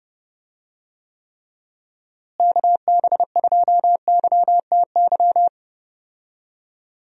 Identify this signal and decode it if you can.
Morse code, heard as a keyed tone, about 30 wpm, 700 Hz. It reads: KB2YTY
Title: USA Call Signs - 6 Characters - Repeated 3x - 30wpm